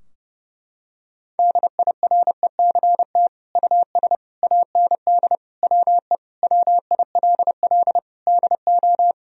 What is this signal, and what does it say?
Morse code, heard as a keyed tone, about 32 wpm, 700 Hz. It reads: DIRECT US AND WE WILL DO
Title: Sentences from Top 1000 Words - Review Part 2 - 30wpm